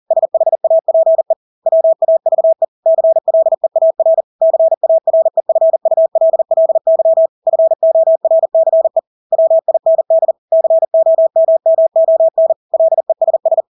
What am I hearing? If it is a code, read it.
Morse code, heard as a keyed tone, about 40 wpm, 650 Hz. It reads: SHAPE WAVE CLEAR CAREFULLY FORCE WIND COMMON LESS
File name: Engelske ord 40-40 wpm